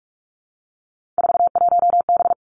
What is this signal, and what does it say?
Morse code, heard as a keyed tone, about 46 wpm, 700 Hz. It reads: 41B
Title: Three Challenging Letter-Number Combinations 45wpm